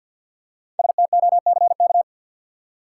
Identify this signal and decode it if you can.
Morse code, heard as a keyed tone, about 53 wpm, 700 Hz. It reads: STOCK
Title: Five Letter Words - 50wpm